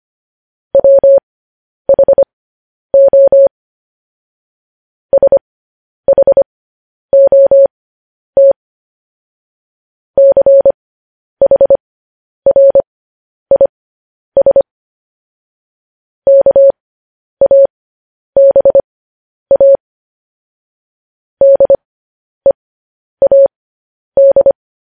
Morse code, heard as a keyed tone, about 25 wpm, spaced as for 10 wpm, 550 Hz. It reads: WHO SHOT CHRIS KABA DEAD
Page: news headlines in morse code at 10 wpm